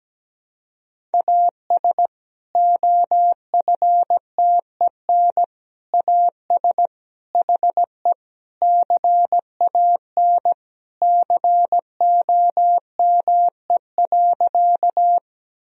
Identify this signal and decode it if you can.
Morse code, heard as a keyed tone, about 17 wpm, 700 Hz. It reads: AS OFTEN AS HE CAN COME.
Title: Sentences from Top 1000 Words - Review Part 1 - 17wpm